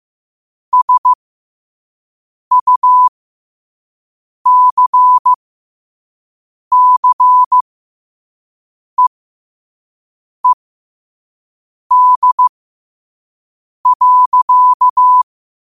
Morse code, heard as a keyed tone, about 15 wpm, spaced as for 5 wpm, 1000 Hz. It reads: SUCCEED.
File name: qotdfiles / 5wpm